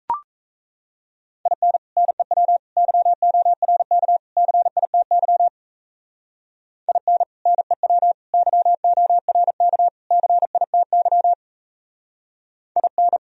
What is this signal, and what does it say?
Morse code, heard as a keyed tone, about 38 wpm, 700 Hz. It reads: IN NEW YORK CITY IN NEW YORK CITY IN